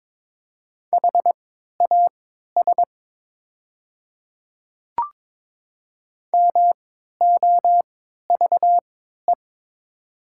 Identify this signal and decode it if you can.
Morse code, heard as a keyed tone, about 22 wpm, spaced as for 12 wpm, 700 Hz. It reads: HAS MOVE
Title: 3x Character Spacing - Top 200 Words - 22wpm